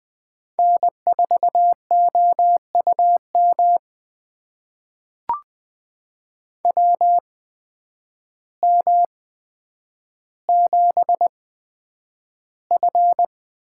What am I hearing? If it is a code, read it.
Morse code, heard as a keyed tone, about 20 wpm, spaced as for 9 wpm, 700 Hz. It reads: N4OUM WM7F
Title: USA Call Signs - 8x Character Spacing - 20wpm